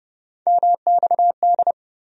Morse code, heard as a keyed tone, about 30 wpm, 700 Hz. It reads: MXD
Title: USA Call Sign Postfixes - 3 Letters 30wpm